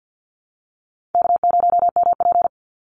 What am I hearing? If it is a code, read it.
Morse code, heard as a keyed tone, about 51 wpm, 700 Hz. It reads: K0MP
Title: NA SOTA Chaser Callsigns 50wpm